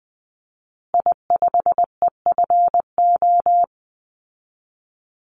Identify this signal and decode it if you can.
Morse code, heard as a keyed tone, about 20 wpm, 700 Hz. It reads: I5EFO
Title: Call Signs Repeated 2x - 20wpm